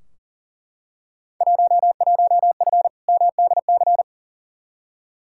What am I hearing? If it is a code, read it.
Morse code, heard as a keyed tone, about 40 wpm, 700 Hz. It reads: 11F MDC
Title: Field Day Exchanges Encore - 40wpm